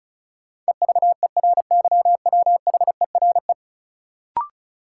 Morse code, heard as a keyed tone, about 36 wpm, 700 Hz. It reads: EVERYWHERE
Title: Sentences from Top 1600 Words - New Word Drill - 35wpm